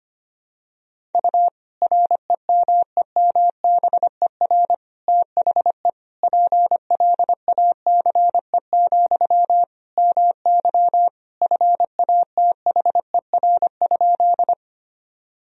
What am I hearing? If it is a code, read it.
Morse code, heard as a keyed tone, about 25 wpm, 700 Hz. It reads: U REMEMBER THE PLACE, MY FATHER?
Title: Sentences from Top 400 Words and Abbreviations 25wpm